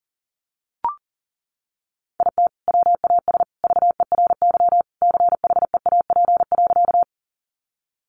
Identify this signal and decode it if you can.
Morse code, heard as a keyed tone, about 40 wpm, 700 Hz. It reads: IT WAS VERY CHEAP.
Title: Sentences from Top 1200 Words 40wpm